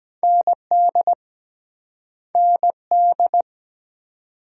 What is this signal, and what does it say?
Morse code, heard as a keyed tone, about 21 wpm, 700 Hz. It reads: ND ND
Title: Speed-Racing Multi-letter Phonemes - 17wpm